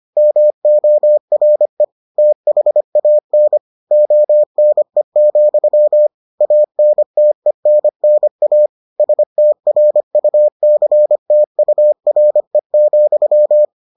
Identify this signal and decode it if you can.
Morse code, heard as a keyed tone, about 25 wpm, 600 Hz. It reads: MORE THAN ONE, ANTENNA STRUCTURE,